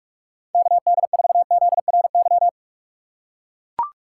Morse code, heard as a keyed tone, about 44 wpm, 700 Hz. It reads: KD4ZRY
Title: USA Call Signs - 6 Characters - 6x Character Spacing - 45wpm